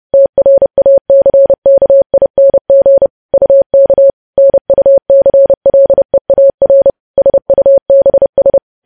Morse code, heard as a keyed tone, about 30 wpm, 550 Hz. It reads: TRACKING UK NUCLEAR SUBS
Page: news headlines in morse code at 30 wpm